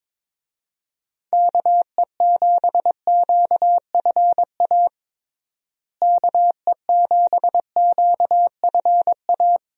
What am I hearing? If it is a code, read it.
Morse code, heard as a keyed tone, about 22 wpm, 700 Hz. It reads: KE7QFA KE7QFA
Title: USA Call Signs - 6 Characters - Repeated 2x - 22wpm